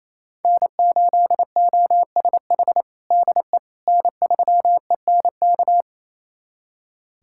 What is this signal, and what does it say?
Morse code, heard as a keyed tone, about 27 wpm, 700 Hz. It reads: N8OSH DE N3ENK
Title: USA Callsign DE Callsign - 5 Characters - 28wpm